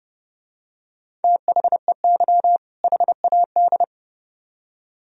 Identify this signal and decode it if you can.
Morse code, heard as a keyed tone, about 27 wpm, 700 Hz. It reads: THEY HAD